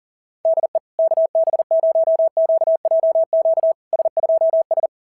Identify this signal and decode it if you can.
Morse code, heard as a keyed tone, about 37 wpm, 650 Hz. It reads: DE KB0QJQ S2S
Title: SOTA QSO Practice 40wpm